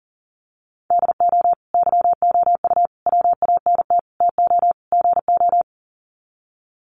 Morse code, heard as a keyed tone, about 40 wpm, 700 Hz. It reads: DO YOU WANT TO GO
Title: Sets of 5 Words Encore2 40wpm